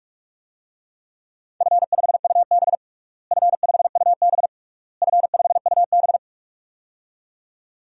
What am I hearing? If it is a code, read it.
Morse code, heard as a keyed tone, about 41 wpm, 700 Hz. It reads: F5UB F5UB F5UB
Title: DXCC-100 Call Signs - 4 Characters - Repeated 3x - 45wpm